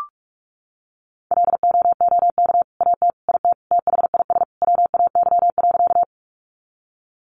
Morse code, heard as a keyed tone, about 44 wpm, 700 Hz. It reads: LOOK AT IT THIS WAY.
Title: Sentences from Top 100 Words 45wpm